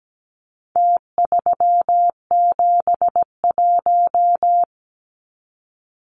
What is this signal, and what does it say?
Morse code, heard as a keyed tone, about 17 wpm, 700 Hz. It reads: T371
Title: Numbers - Contest Style - 4-Digit 17wpm